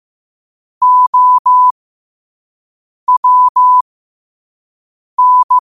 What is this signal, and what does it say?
Morse code, heard as a keyed tone, about 15 wpm, spaced as for 5 wpm, 1000 Hz. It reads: OWN
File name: qotdfiles / 5wpm